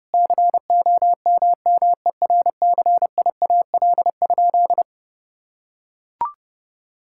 Morse code, heard as a keyed tone, about 30 wpm, 700 Hz. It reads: COMMERCIAL?
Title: Sentences from Top 1000 Words - Review Part 5 - 30wpm